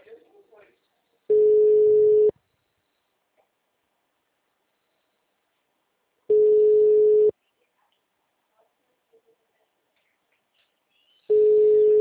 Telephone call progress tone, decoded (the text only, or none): ringback tone